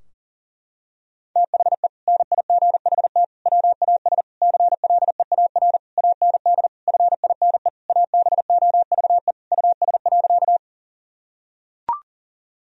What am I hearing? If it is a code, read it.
Morse code, heard as a keyed tone, about 37 wpm, 700 Hz. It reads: THE NIGHT WAS CLEAR AND FINE ABOVE US.
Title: Sentences from Top 1000 Words - Review Part 2 - 40wpm